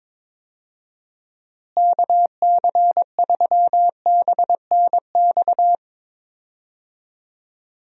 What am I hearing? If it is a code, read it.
Morse code, heard as a keyed tone, about 22 wpm, 700 Hz. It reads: KC3BNX
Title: USA Call Signs - 6 Characters - Repeated 2x - 22wpm